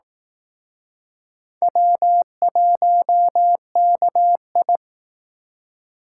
Morse code, heard as a keyed tone, about 19 wpm, 700 Hz. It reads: W1KI
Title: Speed Racing - USA 1x2 Call Signs 15wpm